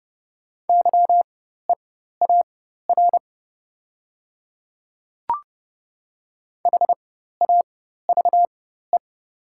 Morse code, heard as a keyed tone, about 30 wpm, spaced as for 14 wpm, 700 Hz. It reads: YEAR HAVE
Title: 4x Character Spacing - Top 200 Words - 30wpm